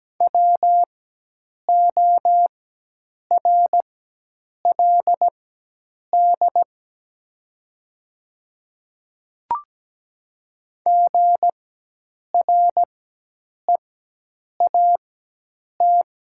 Morse code, heard as a keyed tone, about 17 wpm, spaced as for 8 wpm, 700 Hz. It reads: WORLD GREAT
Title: 4x Character Spacing - Top 500 Words - 17wpm